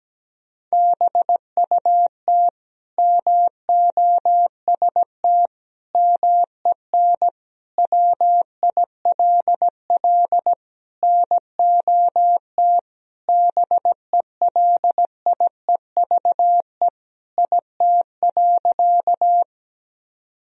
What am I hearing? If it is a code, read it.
Morse code, heard as a keyed tone, about 17 wpm, 700 Hz. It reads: BUT MOST MEN WILL NOT BELIEVE IT.
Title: Sentences from Top 1000 Words - Review Part 1 - 17wpm